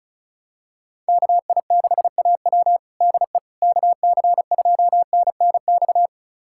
Morse code, heard as a keyed tone, about 34 wpm, 700 Hz. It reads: KI6AW DE KC2NNX
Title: USA Callsign DE Callsign 35wpm